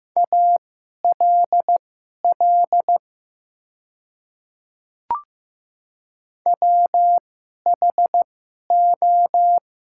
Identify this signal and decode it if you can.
Morse code, heard as a keyed tone, about 15 wpm, spaced as for 11 wpm, 700 Hz. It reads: ALL WHO
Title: 2x Character Spacing - Top 100 Words - 15wpm